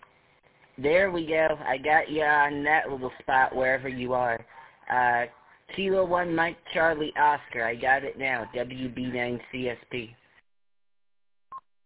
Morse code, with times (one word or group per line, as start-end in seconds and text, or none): none